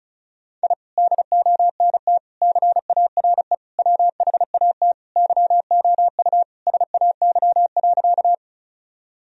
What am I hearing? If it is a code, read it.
Morse code, heard as a keyed tone, about 36 wpm, 700 Hz. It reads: I DONT CARE WHAT YOU SAY.
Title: Sentences from Top 1000 Words - Review Part 1 - 35wpm